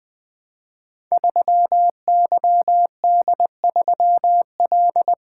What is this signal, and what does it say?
Morse code, heard as a keyed tone, about 20 wpm, 700 Hz. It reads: 3YD3L
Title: Code Groups - 5 Letter:Numbers - 20wpm